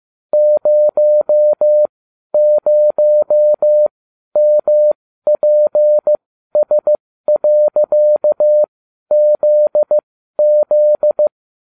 Morse code, heard as a keyed tone, about 15 wpm, spaced as for 12 wpm, 600 Hz. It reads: 00MPS.ZZ